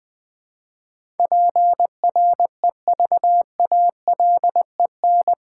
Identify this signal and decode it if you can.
Morse code, heard as a keyed tone, about 20 wpm, 700 Hz. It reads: PREVALEN
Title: Build A Word - 4 Letters At A Time - 10-Letter Words - 20wpm